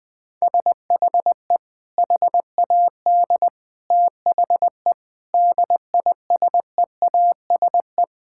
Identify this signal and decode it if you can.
Morse code, heard as a keyed tone, about 20 wpm, 700 Hz. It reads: SHE HAD THE DISEASE
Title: Sentences from Top 1000 Words - Review Part 6 - 20wpm